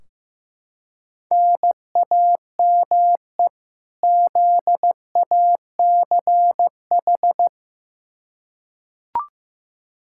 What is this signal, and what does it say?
Morse code, heard as a keyed tone, about 15 wpm, 700 Hz. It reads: NAME ZACH
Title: K1USN SST QSO Practice 15wpm